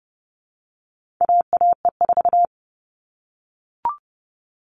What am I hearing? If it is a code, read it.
Morse code, heard as a keyed tone, about 30 wpm, 700 Hz. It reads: AAE4
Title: All Cut Numbers - 4-Digit 30wpm